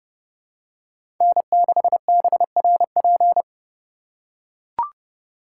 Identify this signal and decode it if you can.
Morse code, heard as a keyed tone, about 29 wpm, 700 Hz. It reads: N6BRP